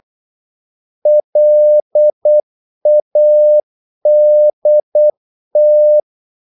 Morse code, heard as a keyed tone, about 8 wpm, 600 Hz. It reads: LADT
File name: Korte ord DK 8wpm